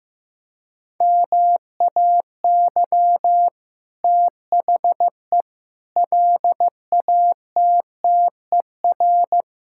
Morse code, heard as a keyed tone, about 15 wpm, 700 Hz. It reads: MAY THE LATTER
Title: Sentences from Top 1600 Words 15wpm